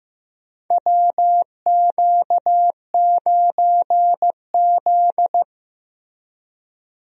Morse code, heard as a keyed tone, about 15 wpm, 700 Hz. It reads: WQ9Z
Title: USA Call Signs - 4 Characters - Repeated 3x - 15wpm